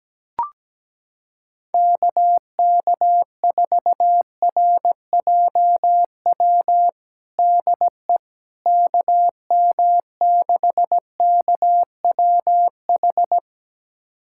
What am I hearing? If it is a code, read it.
Morse code, heard as a keyed tone, about 17 wpm, 700 Hz. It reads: KK4RJW DE KM6KWH
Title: USA Callsign DE Callsign 17wpm